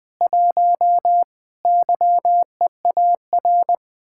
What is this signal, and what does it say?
Morse code, heard as a keyed tone, about 20 wpm, 700 Hz. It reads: 1 YEAR